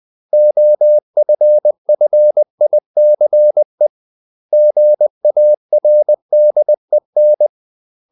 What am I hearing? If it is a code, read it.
Morse code, heard as a keyed tone, about 20 wpm, 600 Hz. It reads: OFFICE GARDEN
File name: CW / Practice / 500MostCommonEnglishWords20